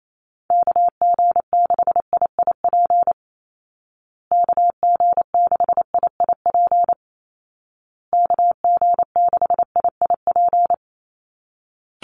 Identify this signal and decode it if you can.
Morse code, heard as a keyed tone, about 28 wpm, 700 Hz. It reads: KG6IIP KG6IIP KG6IIP
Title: Call Signs Repeated 2x - 28wpm